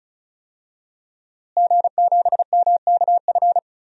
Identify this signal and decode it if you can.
Morse code, heard as a keyed tone, about 36 wpm, 700 Hz. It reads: G7MKF
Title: DXCC-100 Call Signs - 5 Characters - Repeated 2x - 35wpm